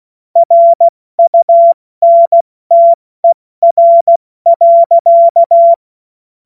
Morse code, heard as a keyed tone, about 16 wpm, spaced as for 14 wpm, 680 Hz. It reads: RUNTER.
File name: Morsekurs / Witze / Witz 30 Januar 2024 70 F80